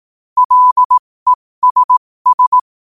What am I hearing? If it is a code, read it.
Morse code, heard as a keyed tone, about 18 wpm, spaced as for 15 wpm, 1000 Hz. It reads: LESS
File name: qotdfiles / 15wpm